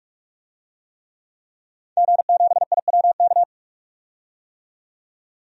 Morse code, heard as a keyed tone, about 46 wpm, 700 Hz. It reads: G7IWK
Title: DXCC-100 Call Signs - 5 Characters - 45wpm